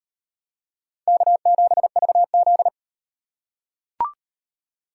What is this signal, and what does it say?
Morse code, heard as a keyed tone, about 38 wpm, 700 Hz. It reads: K7VZ